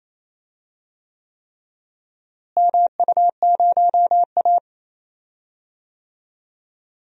Morse code, heard as a keyed tone, about 28 wpm, 700 Hz. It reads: MU0A
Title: DXCC-100 Call Signs - 4 Characters - Repeated 3x - 28wpm